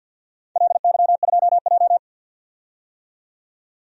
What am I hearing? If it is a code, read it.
Morse code, heard as a keyed tone, about 47 wpm, 700 Hz. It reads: LY2J
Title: DXCC-100 Call Signs - 4 Characters - 50wpm